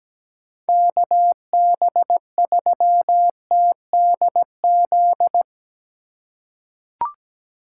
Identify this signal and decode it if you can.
Morse code, heard as a keyed tone, about 17 wpm, 700 Hz. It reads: KB3TDZ
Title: USA Call Signs - 6 Characters - 3x Character Spacing - 17wpm